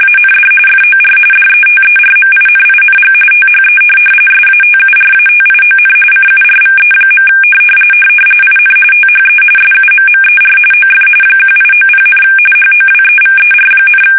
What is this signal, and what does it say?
RTTY, about 45.45 baud. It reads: AND FOR DETERMINING THE SATURATION CURVE. IN THE SUMMER OF 1890, HOWEVER,
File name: fsk850 / benjamin-g-lamme